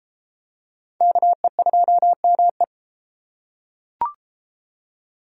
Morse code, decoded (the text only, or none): KE2ME